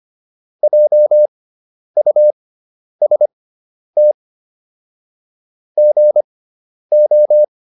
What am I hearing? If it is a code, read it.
Morse code, heard as a keyed tone, about 25 wpm, spaced as for 10 wpm, 600 Hz. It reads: JUST GO